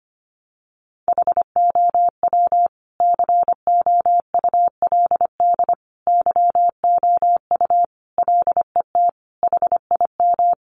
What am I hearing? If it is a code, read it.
Morse code, heard as a keyed tone, about 25 wpm, 700 Hz. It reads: HOW COULD YOU LET HIM
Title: Sentences from Top 300 Words 25wpm